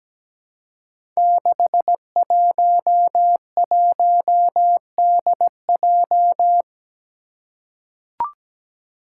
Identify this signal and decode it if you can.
Morse code, heard as a keyed tone, about 17 wpm, 700 Hz. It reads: 611DJ
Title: Five Challenging Letter-Number Combinations 17wpm